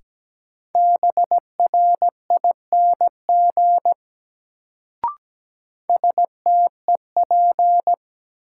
Morse code, heard as a keyed tone, about 17 wpm, 700 Hz. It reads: BRING STEP
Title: Sets of 5 Words Encore - New Word Drill - 17wpm